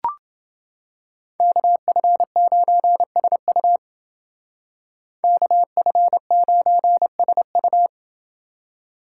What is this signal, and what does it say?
Morse code, heard as a keyed tone, about 29 wpm, 700 Hz. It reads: KF9SU KF9SU